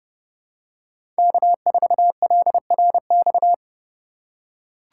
Morse code, heard as a keyed tone, about 30 wpm, 700 Hz. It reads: K4LRX
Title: K1USN SST QSO Practice - 30wpm